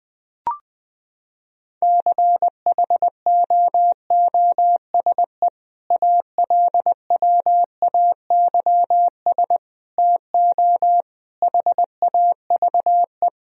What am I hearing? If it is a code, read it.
Morse code, heard as a keyed tone, about 21 wpm, 700 Hz. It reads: CHOOSE ALWAYS TO HAVE
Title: Sentences from Top 1000 Words - Review Part 4 - 20wpm